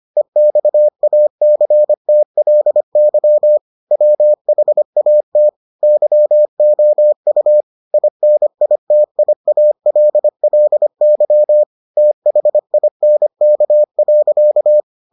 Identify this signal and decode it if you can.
Morse code, heard as a keyed tone, about 25 wpm, 600 Hz. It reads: EXACTLY WHAT YOU INITIALLY THINK.